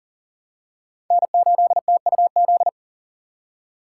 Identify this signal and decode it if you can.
Morse code, heard as a keyed tone, about 39 wpm, 700 Hz. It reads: N8TUZ
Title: USA Call Signs - 5 Characters - 6x Character Spacing - 40wpm